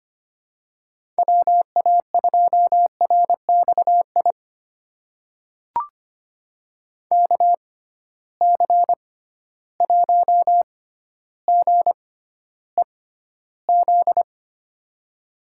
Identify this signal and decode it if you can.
Morse code, heard as a keyed tone, about 25 wpm, spaced as for 13 wpm, 700 Hz. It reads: WA2RXI KC1GEZ
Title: USA Call Signs - 6 Characters - 6x Character Spacing - 25wpm